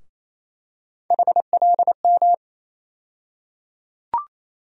Morse code, heard as a keyed tone, about 28 wpm, 700 Hz. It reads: HLM
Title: USA Call Sign Postfixes - 3 Letters 28wpm